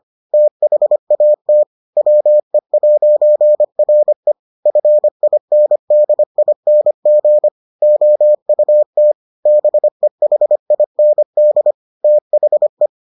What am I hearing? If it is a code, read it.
Morse code, heard as a keyed tone, about 25 wpm, 600 Hz. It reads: THAT WE'RE FINDING OUT BEHIND THE